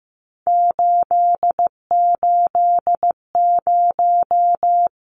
Morse code, heard as a keyed tone, about 15 wpm, 700 Hz. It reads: 880
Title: Numbers - 3-Digit 15wpm